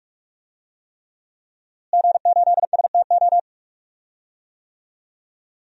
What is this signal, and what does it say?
Morse code, heard as a keyed tone, about 42 wpm, 700 Hz. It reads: G8STO
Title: DXCC-100 Call Signs - 5 Characters - Repeated 3x - 45wpm